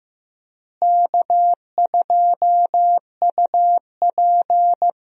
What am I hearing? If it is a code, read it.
Morse code, heard as a keyed tone, about 15 wpm, 700 Hz. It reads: K2UP
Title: USA Call Signs - 4 Characters - Repeated 3x - 15wpm